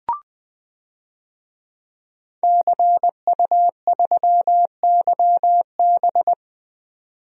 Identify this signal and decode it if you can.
Morse code, heard as a keyed tone, about 20 wpm, 700 Hz. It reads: CU3YB
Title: DXCC-100 Call Signs - 5 Characters - 20wpm